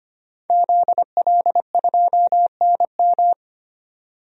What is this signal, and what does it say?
Morse code, heard as a keyed tone, about 25 wpm, 700 Hz. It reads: ZL2NM
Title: DXCC-100 Call Signs - 5 Characters - 25wpm